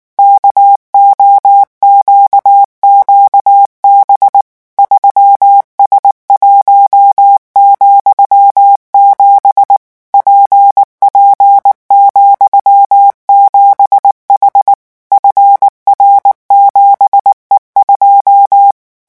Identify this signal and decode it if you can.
Morse code, heard as a keyed tone, about 19 wpm, 800 Hz. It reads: KOQQB 3S1,7 PP,7H FR7E2
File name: morse mp3 / revision13-vitesse 18 mots